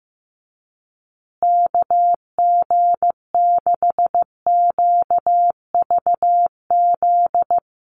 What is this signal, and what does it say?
Morse code, heard as a keyed tone, about 15 wpm, 700 Hz. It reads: KG6QVZ
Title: USA Call Signs Repeated 3x 15wpm